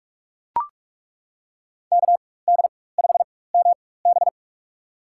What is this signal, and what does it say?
Morse code, heard as a keyed tone, about 42 wpm, spaced as for 21 wpm, 700 Hz. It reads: KD5MB
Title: USA Call Signs - 5 Characters - 4x Character Spacing - 45wpm